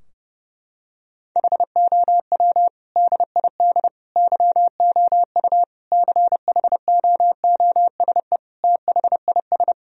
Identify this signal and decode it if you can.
Morse code, heard as a keyed tone, about 30 wpm, 700 Hz. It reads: HOW DID YOU CHOOSE THIS
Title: Sentences from Top 1000 Words - Review Part 4 - 30wpm